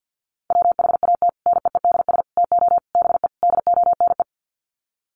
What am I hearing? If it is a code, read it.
Morse code, heard as a keyed tone, about 50 wpm, 700 Hz. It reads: WHAT NEEDS TO BE DONE
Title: Sets of 5 Words Encore 50wpm